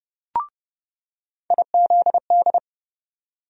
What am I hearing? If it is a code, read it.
Morse code, heard as a keyed tone, about 30 wpm, 700 Hz. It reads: IZD
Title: USA Call Sign Postfixes - 3 Letters 30wpm